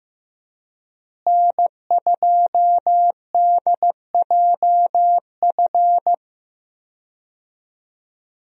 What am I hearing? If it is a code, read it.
Morse code, heard as a keyed tone, about 15 wpm, 700 Hz. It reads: N2DJF
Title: USA Call Signs - 5 Characters - Repeated 3x - 15wpm